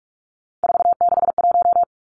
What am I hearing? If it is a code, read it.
Morse code, heard as a keyed tone, about 46 wpm, 700 Hz. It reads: 461
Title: Three Challenging Letter-Number Combinations 45wpm